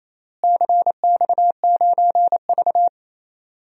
Morse code, heard as a keyed tone, about 28 wpm, 700 Hz. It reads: CX9V
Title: DXCC-100 Call Signs - 4 Characters - Repeated 2x - 28wpm